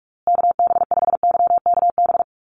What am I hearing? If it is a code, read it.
Morse code, heard as a keyed tone, about 44 wpm, 700 Hz. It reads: KB5YKB
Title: USA Call Signs Repeated 2x 45wpm